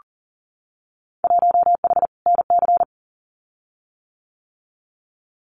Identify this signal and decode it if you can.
Morse code, heard as a keyed tone, about 38 wpm, 700 Hz. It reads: 1H NC
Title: Winter Field Day Exchanges 40wpm